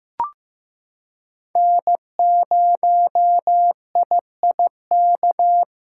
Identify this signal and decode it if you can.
Morse code, heard as a keyed tone, about 15 wpm, 700 Hz. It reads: N0IIK
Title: USA Callsign DE Callsign - 5 Characters - 15wpm